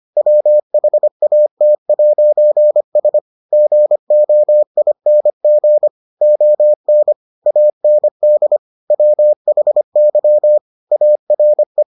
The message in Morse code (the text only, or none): WHAT'S GOING ON AND WHY ARE